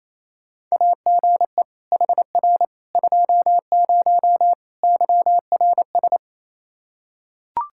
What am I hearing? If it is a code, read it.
Morse code, heard as a keyed tone, about 28 wpm, 700 Hz. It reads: AGE HR 20 YRS
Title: Practice QSO - Exchange 3 - More Info - 28wpm